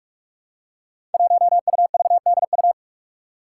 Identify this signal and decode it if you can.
Morse code, heard as a keyed tone, about 44 wpm, 700 Hz. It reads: 1UVDU
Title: Five Challenging Letter-Number Combinations 45wpm